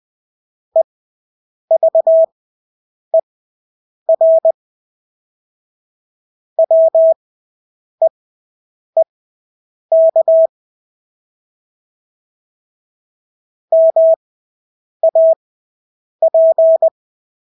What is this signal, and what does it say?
Morse code, heard as a keyed tone, about 20 wpm, spaced as for 8 wpm, 650 Hz. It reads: EVER WEEK MAP
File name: Engelske ord 20/8 wpm